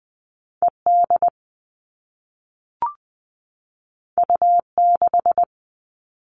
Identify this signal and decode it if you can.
Morse code, heard as a keyed tone, about 20 wpm, 700 Hz. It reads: ED U6
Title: All Cut Numbers - 2-Digit 20wpm